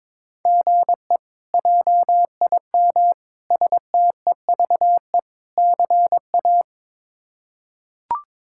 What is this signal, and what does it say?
Morse code, heard as a keyed tone, about 22 wpm, 700 Hz. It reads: GE JIM STEVE CA
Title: K1USN SST QSO Practice - 22wpm